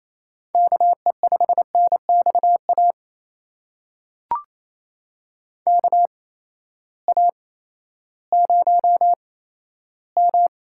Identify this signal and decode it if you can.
Morse code, heard as a keyed tone, about 28 wpm, spaced as for 14 wpm, 700 Hz. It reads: KE5NXA KA0M
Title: USA Call Signs - 6 Characters - 8x Character Spacing - 28wpm